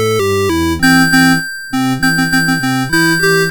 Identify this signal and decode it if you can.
There is no code.